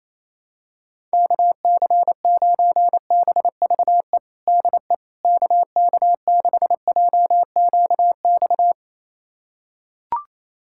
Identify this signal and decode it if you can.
Morse code, heard as a keyed tone, about 28 wpm, 700 Hz. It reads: KC9BVE DE KK6JQX
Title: USA Callsign DE Callsign - 6 Characters - 28wpm